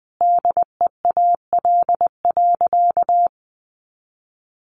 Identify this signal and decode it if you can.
Morse code, heard as a keyed tone, about 20 wpm, 700 Hz. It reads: DEAL.
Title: Sentences from Top 800 Words 20wpm